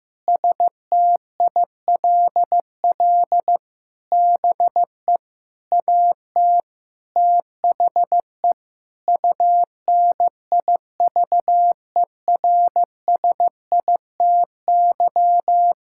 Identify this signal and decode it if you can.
Morse code, heard as a keyed tone, about 15 wpm, 700 Hz. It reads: STILL BE AT THE UNIVERSITY
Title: Sentences from Top 1000 Words - Review Part 5 - 15wpm